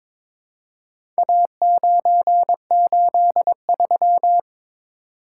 Morse code, heard as a keyed tone, about 22 wpm, 700 Hz. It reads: A983
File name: practice / 22wpm / Four Letter-Number Combinations - 22wpm